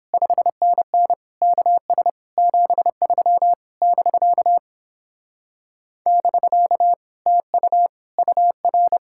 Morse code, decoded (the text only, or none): 5NN KS 73 <BK> <BK> TU UR